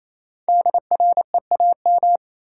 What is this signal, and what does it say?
Morse code, heard as a keyed tone, about 28 wpm, 700 Hz. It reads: DREAM
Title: Five Letter Words - 28wpm